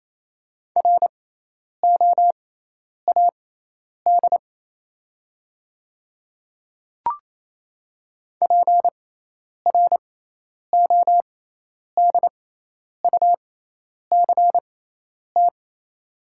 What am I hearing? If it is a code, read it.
Morse code, heard as a keyed tone, about 28 wpm, spaced as for 10 wpm, 700 Hz. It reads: ROAD PRODUCT
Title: 6x Character Spacing - Top 500 Words - 28wpm